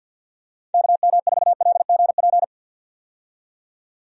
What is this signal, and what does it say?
Morse code, heard as a keyed tone, about 50 wpm, 700 Hz. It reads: KM4LGP
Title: USA Call Signs - 6 Characters - 50wpm